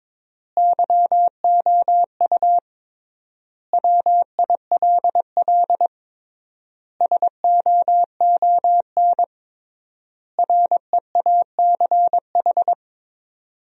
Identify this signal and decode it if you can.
Morse code, heard as a keyed tone, about 22 wpm, 700 Hz. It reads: YOU WILL SOON REACH
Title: 3x Word-Spacing - Sentences from Top 400 Words - 22wpm